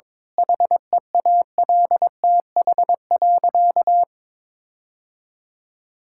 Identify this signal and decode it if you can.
Morse code, heard as a keyed tone, about 22 wpm, 700 Hz. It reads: HEALTH.
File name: Difficult Sentences with Hints - 22wpm